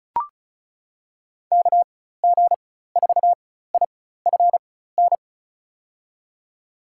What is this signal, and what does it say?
Morse code, heard as a keyed tone, about 34 wpm, spaced as for 16 wpm, 700 Hz. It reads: KG4IFN
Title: USA Call Signs - 6 Characters - 4x Character Spacing - 35wpm